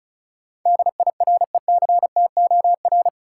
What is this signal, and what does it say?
Morse code, heard as a keyed tone, about 36 wpm, 700 Hz. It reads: DIRECTOR
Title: Sets of 3 Words - New Word Drill - 35wpm